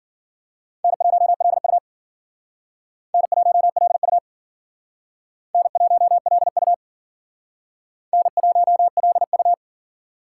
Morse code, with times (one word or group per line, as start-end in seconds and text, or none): 0.8-1.8 N1LU
3.1-4.2 N1LU
5.5-6.7 N1LU
8.1-9.5 N1LU